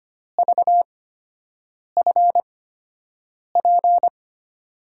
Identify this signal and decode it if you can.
Morse code, heard as a keyed tone, about 25 wpm, spaced as for 7 wpm, 700 Hz. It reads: VFP
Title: USA Call Signs - 6 Characters - 8x Character Spacing - 25wpm